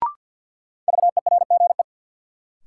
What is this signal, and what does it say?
Morse code, heard as a keyed tone, about 48 wpm, 700 Hz. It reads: VERGE
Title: Sets of 4 Words - New Word Drill - 50wpm